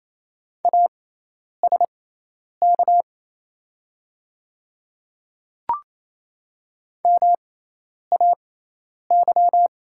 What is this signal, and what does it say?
Morse code, heard as a keyed tone, about 27 wpm, spaced as for 10 wpm, 700 Hz. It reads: ASK MAY
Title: 6x Character Spacing - Top 300 Words - 28wpm